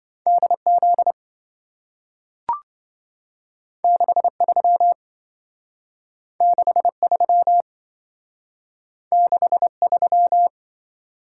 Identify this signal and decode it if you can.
Morse code, heard as a keyed tone, about 27 wpm, 700 Hz. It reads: DZ 63 63 63